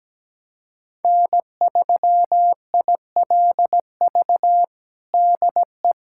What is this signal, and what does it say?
Morse code, heard as a keyed tone, about 17 wpm, 700 Hz. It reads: N3ILV DE
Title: USA Callsign DE Callsign 17wpm